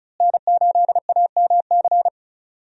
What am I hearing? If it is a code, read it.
Morse code, heard as a keyed tone, about 36 wpm, 700 Hz. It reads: N8AMC
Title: USA Call Signs - 5 Characters - Repeated 3x - 35wpm